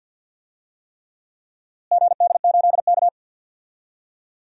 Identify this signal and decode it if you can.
Morse code, heard as a keyed tone, about 51 wpm, 700 Hz. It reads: GD8K
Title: DXCC-100 Call Signs - 4 Characters - Repeated 3x - 50wpm